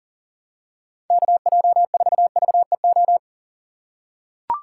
Morse code, heard as a keyed tone, about 40 wpm, 700 Hz. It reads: KJ4VEO
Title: USA Call Signs - 3x Character Spacing - 40wpm